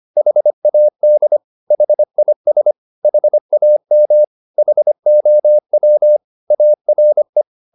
Morse code, heard as a keyed tone, about 25 wpm, 600 Hz. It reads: HAD HIS HAM HOW ARE